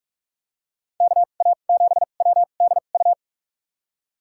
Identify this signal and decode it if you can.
Morse code, heard as a keyed tone, about 46 wpm, spaced as for 32 wpm, 700 Hz. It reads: KA7WDU
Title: USA Call Signs - 6 Characters - 2x Character Spacing - 45wpm